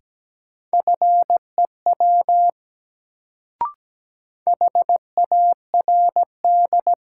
Morse code, heard as a keyed tone, about 17 wpm, 700 Hz. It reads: FEW HARD
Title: Sentences from Top 300 Words - New Word Drill - 17wpm